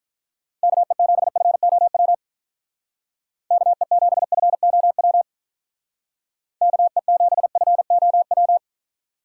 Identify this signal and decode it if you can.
Morse code, heard as a keyed tone, about 46 wpm, 700 Hz. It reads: KE7FOW KE7FOW KE7FOW